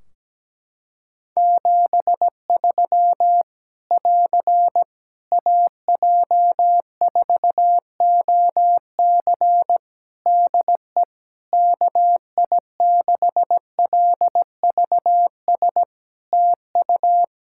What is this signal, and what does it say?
Morse code, heard as a keyed tone, about 17 wpm, 700 Hz. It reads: 73 <AR> AJ4OC DE KI6LVS TU
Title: Practice QSO - Exchange 4 - Ending - 17wpm